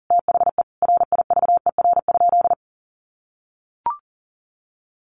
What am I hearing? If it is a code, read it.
Morse code, heard as a keyed tone, about 40 wpm, 700 Hz. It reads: THE RIVER?
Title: Sentences from Top 1400 Words 40wpm